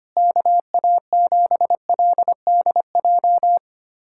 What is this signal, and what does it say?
Morse code, heard as a keyed tone, about 25 wpm, 700 Hz. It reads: KA7LDJ
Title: USA Callsign DE Callsign - 6 Characters - 25wpm